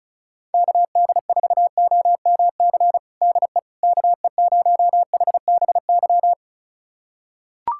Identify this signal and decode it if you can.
Morse code, heard as a keyed tone, about 34 wpm, 700 Hz. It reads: KD4OMC DE KE0HBY
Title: USA Callsign DE Callsign - 6 Characters - 35wpm